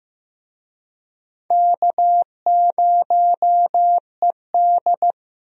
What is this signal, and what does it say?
Morse code, heard as a keyed tone, about 15 wpm, 700 Hz. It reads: K0ED
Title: USA Call Signs - 4 Characters - Repeated 3x - 15wpm